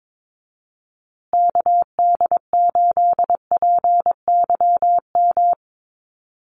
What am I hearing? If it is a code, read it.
Morse code, heard as a keyed tone, about 22 wpm, 700 Hz. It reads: KD8PYM
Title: USA Call Signs Repeated 2x 22wpm